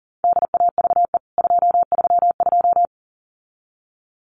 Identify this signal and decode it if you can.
Morse code, heard as a keyed tone, about 40 wpm, 700 Hz. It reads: DAVE 232
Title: CWT Exchange 40wpm